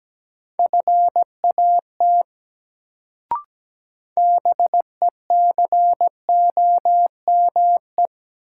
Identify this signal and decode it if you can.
Morse code, heard as a keyed tone, about 17 wpm, 700 Hz. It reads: FAT BECOME
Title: Sentences from Top 600 Words - New Word Drill - 17wpm